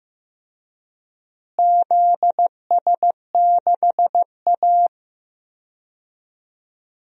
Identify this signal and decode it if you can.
Morse code, heard as a keyed tone, about 15 wpm, 700 Hz. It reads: ZS6A
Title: DXCC-100 Call Signs - 4 Characters - Repeated 3x - 15wpm